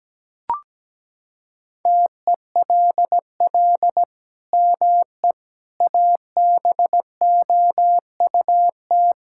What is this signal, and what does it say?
Morse code, heard as a keyed tone, about 17 wpm, 700 Hz. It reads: TELL ME ABOUT
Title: Sentences from Top 1600 Words 17wpm